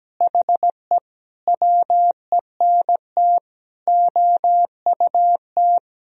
Morse code, heard as a keyed tone, about 17 wpm, 700 Hz. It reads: HE WENT OUT
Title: Sentences from Top 1000 Words - Review Part 2 - 17wpm